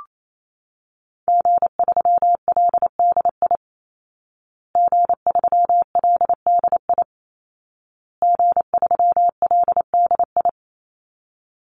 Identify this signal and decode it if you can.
Morse code, heard as a keyed tone, about 28 wpm, 700 Hz. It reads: G3LDI G3LDI G3LDI
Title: Call Signs Repeated 2x - 28wpm